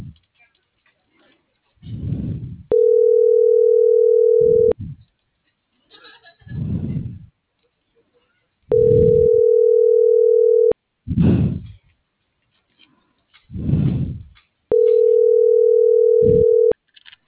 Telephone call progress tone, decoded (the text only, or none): ringback tone